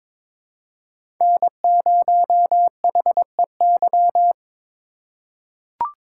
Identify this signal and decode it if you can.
Morse code, heard as a keyed tone, about 22 wpm, 700 Hz. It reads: N0HEY
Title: USA Call Signs - 5 Characters - 4x Character Spacing - 22wpm